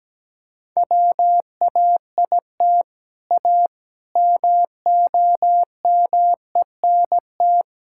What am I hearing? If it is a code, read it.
Morse code, heard as a keyed tone, about 17 wpm, 700 Hz. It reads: WAIT A MOMENT
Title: Sentences from Top 1000 Words - Review Part 5 - 17wpm